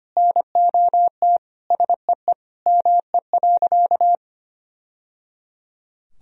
Morse code, heard as a keyed tone, about 25 wpm, 700 Hz. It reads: NOT SEE ME.
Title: Sentences from Top 1000 Words - Review Part 3 - 25wpm